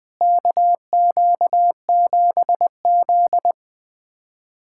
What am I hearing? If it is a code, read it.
Morse code, heard as a keyed tone, about 20 wpm, 700 Hz. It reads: KQ7Z
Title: USA Callsign DE Callsign - 4 Characters - 20wpm